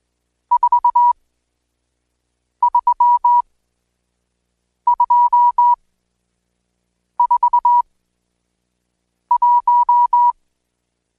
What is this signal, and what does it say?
Morse code, heard as a keyed tone, about 20 wpm, spaced as for 5 wpm, 970 Hz. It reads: 43241
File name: Code Quick / AudioDownld1 / Seg76-Day8-DayDrill 8